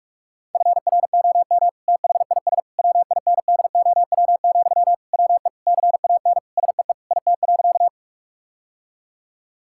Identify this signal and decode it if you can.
Morse code, heard as a keyed tone, about 42 wpm, 700 Hz. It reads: FROM THIS WINDOW, WE CAN SEE IT.
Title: Sentences from Top 1600 Words 45wpm